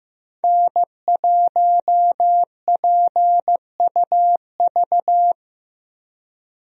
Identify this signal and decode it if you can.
Morse code, heard as a keyed tone, about 16 wpm, 700 Hz. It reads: N1PUV
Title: USA Call Signs - 5 Characters - Repeated 2x - 15wpm